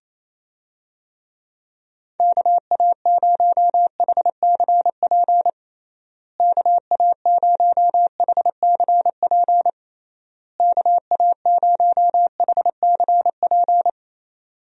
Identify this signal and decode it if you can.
Morse code, heard as a keyed tone, about 28 wpm, 700 Hz. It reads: KA0HCP KA0HCP KA0HCP
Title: USA Call Signs - 6 Characters - Repeated 3x - 28wpm